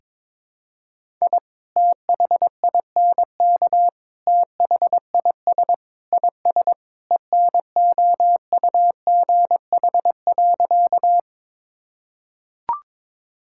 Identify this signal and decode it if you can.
Morse code, heard as a keyed tone, about 22 wpm, 700 Hz. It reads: I THINK THIS IS ENOUGH.
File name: practice / 22wpm / Sentences with 4-Word Phrases - 22wpm